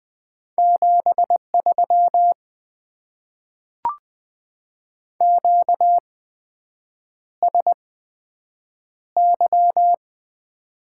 Morse code, heard as a keyed tone, about 20 wpm, spaced as for 7 wpm, 700 Hz. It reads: 73 QSY
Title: Top 90 QSO Elements - 8x Character Spacing - 20wpm